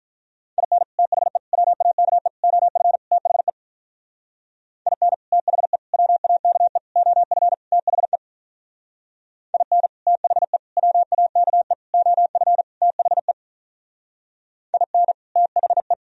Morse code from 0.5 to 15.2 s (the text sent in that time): IN THE WAKE OF THE IN THE WAKE OF THE IN THE WAKE OF THE IN